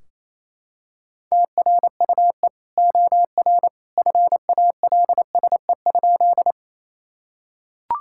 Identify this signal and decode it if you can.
Morse code, heard as a keyed tone, about 28 wpm, 700 Hz. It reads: TRUE OR FALSE?
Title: Sentences from Top 1600 Words 28wpm